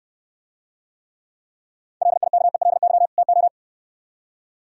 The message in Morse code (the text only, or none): LIKELY TO